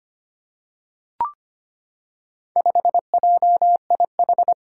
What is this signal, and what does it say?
Morse code, heard as a keyed tone, about 25 wpm, 700 Hz. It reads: 5JIH
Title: Four Challenging Letter-Number Combinations 25wpm